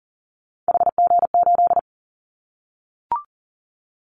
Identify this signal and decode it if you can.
Morse code, heard as a keyed tone, about 40 wpm, 700 Hz. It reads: HG8
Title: Callsign Prefixes for Top 25 DXCC - Repeated 3x - 40wpm